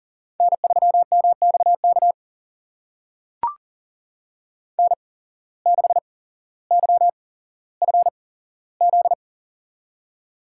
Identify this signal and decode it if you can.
Morse code, heard as a keyed tone, about 40 wpm, spaced as for 17 wpm, 700 Hz. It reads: N3MXK N6YFZ
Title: USA Call Signs - 5 Characters - 8x Character Spacing - 40wpm